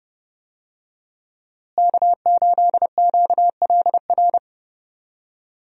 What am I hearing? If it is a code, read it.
Morse code, heard as a keyed tone, about 29 wpm, 700 Hz. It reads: K8QLR
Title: USA Call Signs - 5 Characters - Repeated 3x - 30wpm